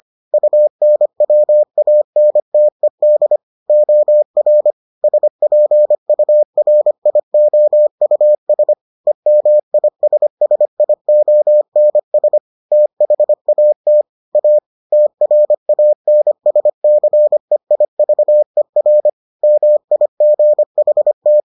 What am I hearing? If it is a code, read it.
Morse code, heard as a keyed tone, about 25 wpm, 600 Hz. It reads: UNWANTED OR SPURIOUS EMISSIONS THAT A TRANSCEIVER MIGHT